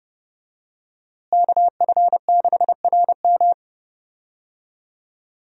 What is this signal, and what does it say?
Morse code, heard as a keyed tone, about 30 wpm, 700 Hz. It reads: KF6RM
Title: USA Call Signs - 5 Characters - Repeated 3x - 30wpm